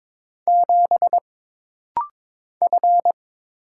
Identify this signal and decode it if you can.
Morse code, heard as a keyed tone, about 22 wpm, 700 Hz. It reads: 7 F
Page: This rapid-fire timing is geared to push your ICR (Instant Character Recognition) skills.